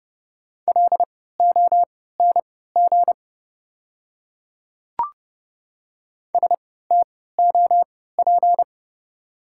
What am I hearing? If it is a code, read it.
Morse code, heard as a keyed tone, about 31 wpm, spaced as for 17 wpm, 690 Hz. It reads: LONG STOP
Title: 3x Character Spacing - Top 300 Words - 30wpm